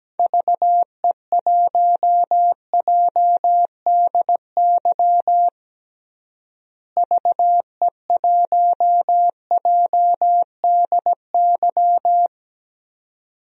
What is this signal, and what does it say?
Morse code, heard as a keyed tone, about 17 wpm, 700 Hz. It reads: VE1JDY VE1JDY
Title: DXCC-100 Call Signs - 6 Characters - Repeated 2x - 17wpm